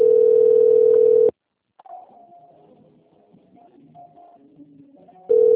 Ringback tone (North America), 440 + 480 Hz.
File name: out- - - - - .85404